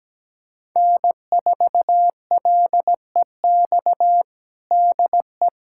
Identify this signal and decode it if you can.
Morse code, heard as a keyed tone, about 17 wpm, 700 Hz. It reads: N4LEX DE
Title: USA Callsign DE Callsign - 5 Characters - 17wpm